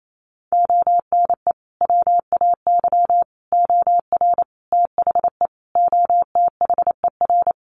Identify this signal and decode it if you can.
Morse code, heard as a keyed tone, about 28 wpm, 700 Hz. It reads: ONE WAY OR THE OTHER
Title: Sets of 5 Words 28wpm